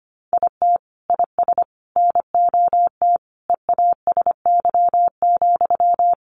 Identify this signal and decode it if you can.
Morse code, heard as a keyed tone, about 25 wpm, 700 Hz. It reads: IT IS NOT EASY,
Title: Sentences from Top 700 Words 25wpm